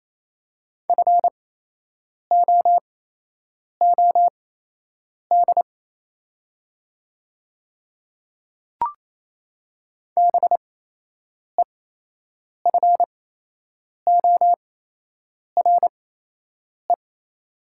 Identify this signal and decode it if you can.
Morse code, heard as a keyed tone, about 28 wpm, spaced as for 8 wpm, 700 Hz. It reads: FOOD BEFORE
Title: 8x Character Spacing - Top 300 Words - 28wpm